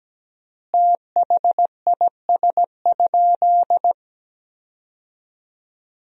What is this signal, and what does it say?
Morse code, heard as a keyed tone, about 17 wpm, 700 Hz. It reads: THIS?
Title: 3x Word-Spacing - Sentences from Top 400 Words - 17wpm